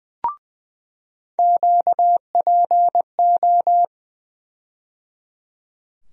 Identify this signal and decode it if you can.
Morse code, heard as a keyed tone, about 20 wpm, 700 Hz. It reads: QPO
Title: USA Call Sign Postfixes - 3 Letters 20wpm